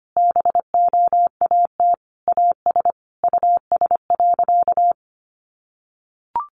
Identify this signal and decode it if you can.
Morse code, heard as a keyed tone, about 25 wpm, 700 Hz. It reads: BOAT AS US.
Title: Sentences from Top 500 Words 25wpm